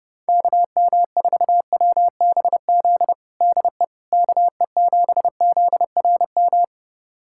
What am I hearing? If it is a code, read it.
Morse code, heard as a keyed tone, about 32 wpm, 700 Hz. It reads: KM4WBZ DE KE7ZRM
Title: USA Callsign DE Callsign 30wpm